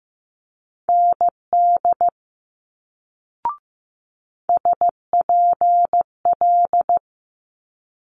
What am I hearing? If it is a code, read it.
Morse code, heard as a keyed tone, about 15 wpm, 700 Hz. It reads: ND SPL
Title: Multi-letter Phonemes Quick Fire 15wpm